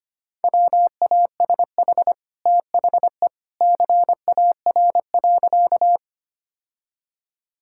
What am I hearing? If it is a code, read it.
Morse code, heard as a keyed tone, about 25 wpm, 690 Hz. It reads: WASH THE CAR.
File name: Easy Sentences with Hints - 25wpm